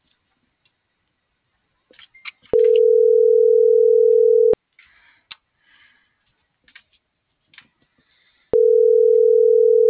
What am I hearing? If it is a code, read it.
Ringback tone (North America), 440 + 480 Hz.